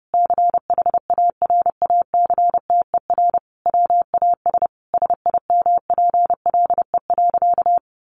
Morse code, heard as a keyed tone, about 30 wpm, 700 Hz. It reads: CHARACTER WAS SIMPLE.
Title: Sentences from Top 1200 Words 30wpm